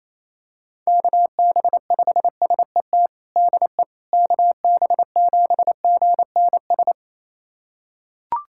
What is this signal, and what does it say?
Morse code, heard as a keyed tone, about 28 wpm, 700 Hz. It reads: KB5SET DE KB7GNS
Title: USA Callsign DE Callsign - 6 Characters - 28wpm